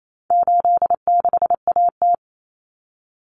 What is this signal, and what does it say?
Morse code, heard as a keyed tone, about 28 wpm, 700 Hz. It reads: 86AT
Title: Common Cut Numbers - 4-Digit 28wpm